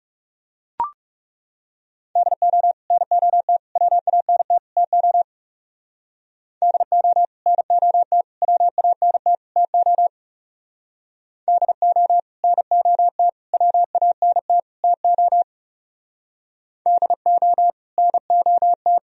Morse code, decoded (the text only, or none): DO NOT WANT TO DO NOT WANT TO DO NOT WANT TO DO NOT